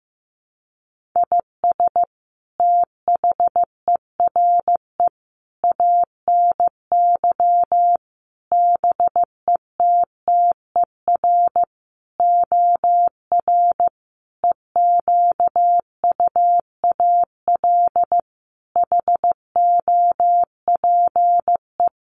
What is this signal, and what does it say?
Morse code, heard as a keyed tone, about 15 wpm, 700 Hz. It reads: IS THERE ANY BETTER OR EQUAL HOPE
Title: Sentences from Top 1400 Words 15wpm